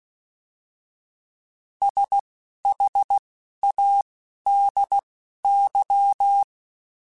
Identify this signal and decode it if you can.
Morse code, heard as a keyed tone, about 16 wpm, spaced as for 11 wpm, 800 Hz. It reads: SHADY